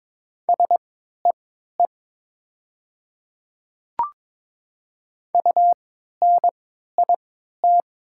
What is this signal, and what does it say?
Morse code, heard as a keyed tone, about 22 wpm, spaced as for 12 wpm, 700 Hz. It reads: SEE UNIT
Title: 3x Character Spacing - Top 500 Words - 22wpm